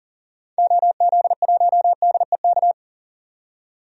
Morse code, heard as a keyed tone, about 40 wpm, 700 Hz. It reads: OZ1DEK
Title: DXCC-100 Call Signs - 6 Characters - 40wpm